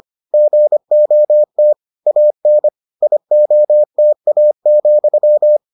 Morse code, heard as a keyed tone, about 25 wpm, 600 Hz. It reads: GOT AN IOTA,